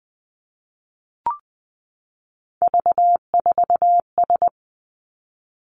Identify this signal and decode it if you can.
Morse code, heard as a keyed tone, about 20 wpm, 700 Hz. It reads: V4S
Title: Three Challenging Letter-Number Combinations 20wpm